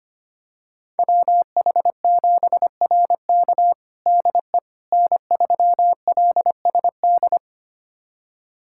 Morse code, heard as a keyed tone, about 25 wpm, 700 Hz. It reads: WH7RK DE N3LSD
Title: USA Callsign DE Callsign - 5 Characters - 25wpm